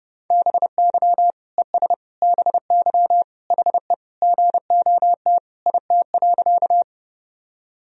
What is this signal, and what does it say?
Morse code, heard as a keyed tone, about 30 wpm, 700 Hz. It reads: BY ES BY HE GOT IT.
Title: Sentences from Top 300 Words and Abbreviations 30wpm